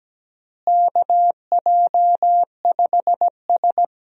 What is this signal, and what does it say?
Morse code, heard as a keyed tone, about 17 wpm, 700 Hz. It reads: KJ5S
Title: USA Call Signs - 4 Characters - Repeated 2x - 17wpm